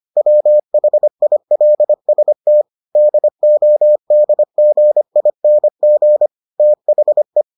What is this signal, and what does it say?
Morse code, heard as a keyed tone, about 25 wpm, 600 Hz. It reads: WHILST DODGING THE